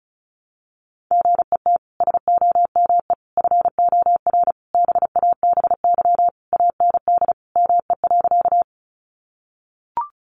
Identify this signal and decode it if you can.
Morse code, heard as a keyed tone, about 34 wpm, 700 Hz. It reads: GET SOME FOR BABY AND ME.
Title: Sentences from Top 1300 Words 35wpm